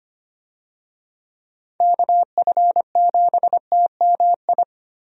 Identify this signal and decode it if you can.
Morse code, heard as a keyed tone, about 25 wpm, 700 Hz. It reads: KF7TMI
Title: USA Call Signs - 6 Characters - Repeated 3x - 25wpm